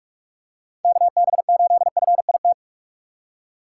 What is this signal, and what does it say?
Morse code, heard as a keyed tone, about 46 wpm, 700 Hz. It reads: KB8FIT
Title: USA Call Signs - 6 Characters - 6x Character Spacing - 45wpm